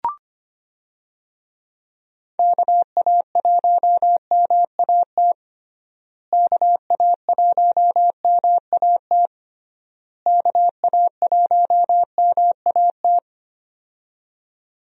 Morse code, heard as a keyed tone, about 25 wpm, 700 Hz. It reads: KA1MAT KA1MAT KA1MAT
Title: USA Call Signs - 6 Characters - Repeated 3x - 25wpm